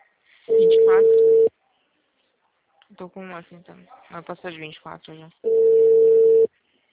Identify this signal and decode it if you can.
Ringback tone (China, 450 Hz), with speech.